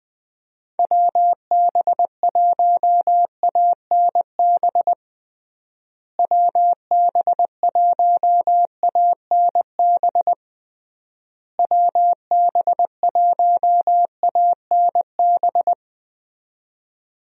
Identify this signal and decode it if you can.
Morse code, heard as a keyed tone, about 20 wpm, 700 Hz. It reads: WB1ANB WB1ANB WB1ANB
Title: USA Call Signs - 6 Characters - Repeated 3x - 20wpm